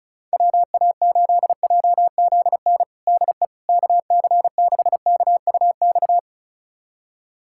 Morse code, heard as a keyed tone, about 36 wpm, 700 Hz. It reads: WA8JZN DE KC6KUX
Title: USA Callsign DE Callsign - 6 Characters - 35wpm